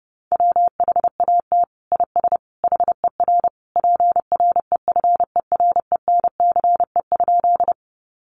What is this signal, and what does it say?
Morse code, heard as a keyed tone, about 30 wpm, 700 Hz. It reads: WHAT IS HER PREFERENCE?
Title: Sentences from Top 1500 Words 30wpm